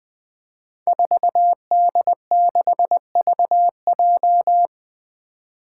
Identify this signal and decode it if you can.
Morse code, heard as a keyed tone, about 21 wpm, 700 Hz. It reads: 4D6VJ
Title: Five Challenging Letter-Number Combinations 20wpm